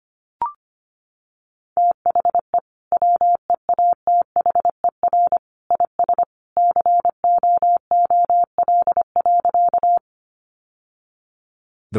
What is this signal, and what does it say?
Morse code, heard as a keyed tone, about 25 wpm, 700 Hz. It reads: THE WEATHER IS COOL.
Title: Sentences from Top 1200 Words 25wpm